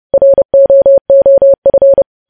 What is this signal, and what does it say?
Morse code, heard as a keyed tone, about 30 wpm, 550 Hz. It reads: ROOF
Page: news headlines in morse code at 30 wpm